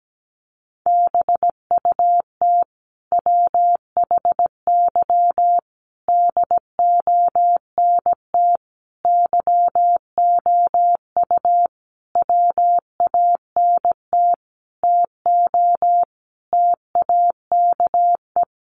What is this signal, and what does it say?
Morse code, heard as a keyed tone, about 17 wpm, 700 Hz. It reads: BUT WHY DONT YOU WANT TO TAKE
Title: Sentences from Top 300 Words 17wpm